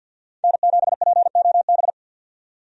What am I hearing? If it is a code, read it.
Morse code, heard as a keyed tone, about 53 wpm, 700 Hz. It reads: N7POB
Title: USA Call Signs - 5 Characters - 50wpm